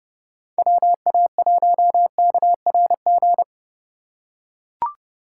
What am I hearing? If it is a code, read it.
Morse code, heard as a keyed tone, about 30 wpm, 700 Hz. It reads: WA1KRG